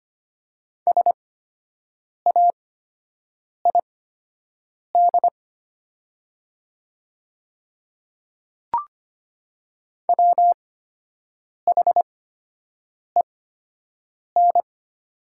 Morse code, heard as a keyed tone, about 25 wpm, spaced as for 7 wpm, 700 Hz. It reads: SAID WHEN
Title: 8x Character Spacing - Top 100 Words - 25wpm